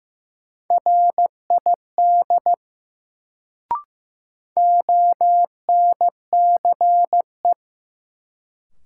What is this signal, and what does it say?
Morse code, heard as a keyed tone, about 15 wpm, 700 Hz. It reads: RID ONCE
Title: Sets of 4 Words - New Word Drill - 15wpm